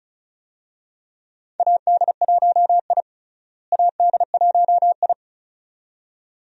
Morse code, heard as a keyed tone, about 34 wpm, 700 Hz. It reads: AD1I AD1I
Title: USA Call Signs - 4 Characters - Repeated 2x - 35wpm